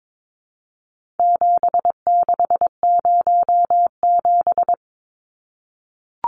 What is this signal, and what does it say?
Morse code, heard as a keyed tone, about 22 wpm, 700 Hz. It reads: 7607
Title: Numbers - 4-Digit 22wpm